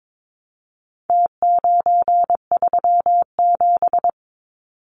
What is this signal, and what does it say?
Morse code, heard as a keyed tone, about 22 wpm, 700 Hz. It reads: T937
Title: Numbers - Contest Style - 4-Digit 22wpm